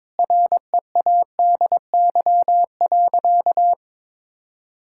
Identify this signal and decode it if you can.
Morse code, heard as a keyed tone, about 22 wpm, 700 Hz. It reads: READY.
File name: practice / 22wpm / Sentences with 2-Word Phrases - 22wpm